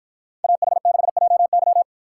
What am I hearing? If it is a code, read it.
Morse code, heard as a keyed tone, about 49 wpm, 700 Hz. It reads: AH6JY